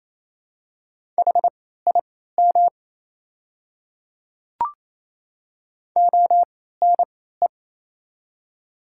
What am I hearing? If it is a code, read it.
Morse code, heard as a keyed tone, about 27 wpm, spaced as for 16 wpm, 700 Hz. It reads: HIM ONE
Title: 3x Character Spacing - Top 300 Words - 28wpm